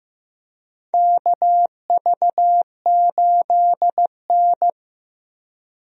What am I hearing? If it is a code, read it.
Morse code, heard as a keyed tone, about 15 wpm, 700 Hz. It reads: KV8N
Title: USA Call Signs - 4 Characters - Repeated 3x - 15wpm